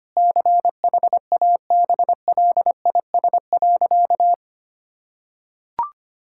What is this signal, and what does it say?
Morse code, heard as a keyed tone, about 25 wpm, 700 Hz. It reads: CHABLIS.
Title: Pangram Challenge v2 - 25wpm